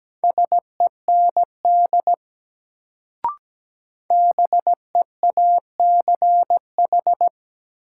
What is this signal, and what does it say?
Morse code, heard as a keyed tone, about 17 wpm, 700 Hz. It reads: SEND BEACH
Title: Sentences from Top 1500 Words - New Word Drill - 17wpm